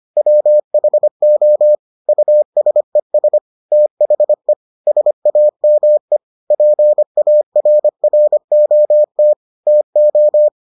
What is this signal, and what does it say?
Morse code, heard as a keyed tone, about 25 wpm, 600 Hz. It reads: WHO USES THE SAME PARROT TO